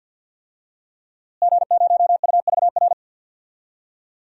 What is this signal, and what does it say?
Morse code, heard as a keyed tone, about 52 wpm, 700 Hz. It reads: G0UVR
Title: DXCC-100 Call Signs - 5 Characters - Repeated 3x - 50wpm